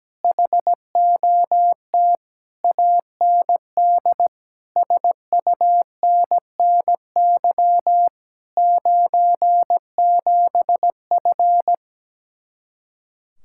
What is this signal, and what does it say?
Morse code, heard as a keyed tone, about 17 wpm, 700 Hz. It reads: HOT AND SUNNY 97F
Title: Speed-Racing Common QSO Phrases - 17wpm